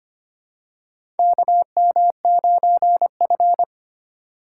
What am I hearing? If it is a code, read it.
Morse code, heard as a keyed tone, about 25 wpm, 700 Hz. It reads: KM9F
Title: USA Call Signs - 4 Characters - 3x Character Spacing - 25wpm